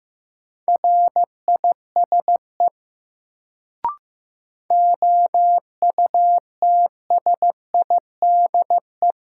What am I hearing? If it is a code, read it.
Morse code, heard as a keyed tone, about 16 wpm, 700 Hz. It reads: RISE OUTSIDE
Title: Sentences from Top 900 Words - New Word Drill - 15wpm